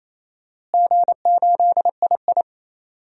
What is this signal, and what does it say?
Morse code, heard as a keyed tone, about 27 wpm, 700 Hz. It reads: G8II
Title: DXCC-100 Call Signs - 4 Characters - Repeated 3x - 28wpm